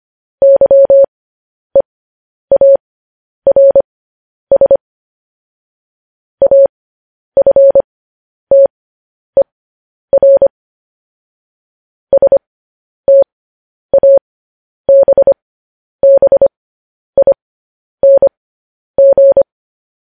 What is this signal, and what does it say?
Morse code, heard as a keyed tone, about 25 wpm, spaced as for 10 wpm, 550 Hz. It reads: YEARS AFTER STABBING
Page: news headlines in morse code at 10 wpm